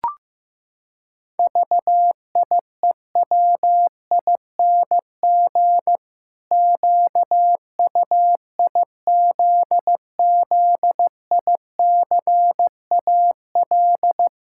Morse code, heard as a keyed tone, about 15 wpm, 700 Hz. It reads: VIEWING QUIZZICAL
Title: Pangram Challenge v2 - 15wpm